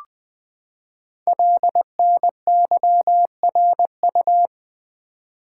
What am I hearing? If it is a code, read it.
Morse code, heard as a keyed tone, about 20 wpm, 700 Hz. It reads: LNYRU
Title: Code Groups - 5 Letter:Numbers - 20wpm